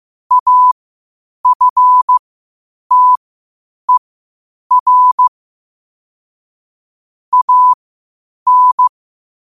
Morse code, heard as a keyed tone, about 15 wpm, spaced as for 8 wpm, 1000 Hz. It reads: AFTER AN
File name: qotdfiles / 8wpm